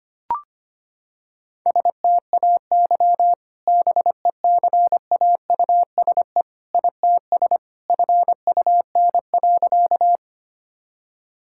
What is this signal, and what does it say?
Morse code, heard as a keyed tone, about 25 wpm, 700 Hz. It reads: STAY BECAUSE ITS FUN.
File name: practice / 25wpm / Sentences with 2-Word Phrases - 25wpm